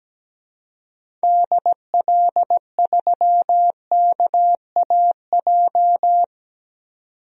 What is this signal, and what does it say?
Morse code, heard as a keyed tone, about 17 wpm, 700 Hz. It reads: DL3KAJ
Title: DXCC-100 Call Signs - 6 Characters - 17wpm